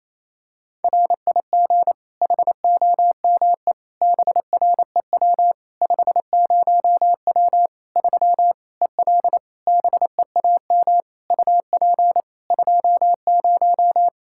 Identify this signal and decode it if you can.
Morse code, heard as a keyed tone, about 28 wpm, 700 Hz. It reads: RIG HOME BREW 50W 3 EL BEAM UP 20
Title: Practice QSO - Exchange 2 - Info - 28wpm